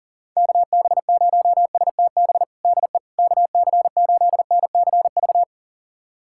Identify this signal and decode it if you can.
Morse code, heard as a keyed tone, about 40 wpm, 700 Hz. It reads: KB0STB DE KC8NCV
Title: USA Callsign DE Callsign - 6 Characters - 40wpm